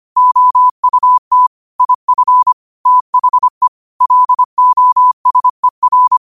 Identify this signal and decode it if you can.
Morse code, heard as a keyed tone, about 25 wpm, 1000 Hz. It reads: OUT IF THE LOSER